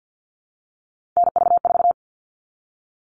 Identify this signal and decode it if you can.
Morse code, heard as a keyed tone, about 50 wpm, 700 Hz. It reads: NV4
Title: All Cut Numbers - 3-Digit 50wpm